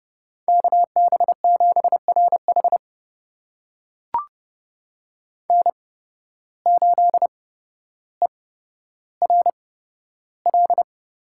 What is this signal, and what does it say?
Morse code, heard as a keyed tone, about 30 wpm, spaced as for 13 wpm, 700 Hz. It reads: KB7RH N8ERL
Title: USA Call Signs - 8x Character Spacing - 30wpm